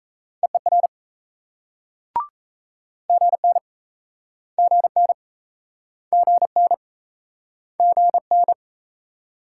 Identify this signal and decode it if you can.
Morse code, heard as a keyed tone, about 40 wpm, 700 Hz. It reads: EER GN GN GN GN